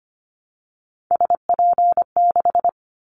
Morse code, heard as a keyed tone, about 25 wpm, 700 Hz. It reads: SP6
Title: Callsign Prefixes for Top 25 DXCC - 25wpm